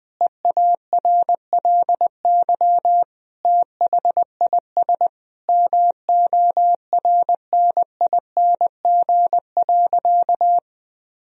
Morse code, heard as a keyed tone, about 20 wpm, 700 Hz. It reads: EARLY THIS MORNING.
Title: Sentences from Top 1000 Words - Review Part 2 - 20wpm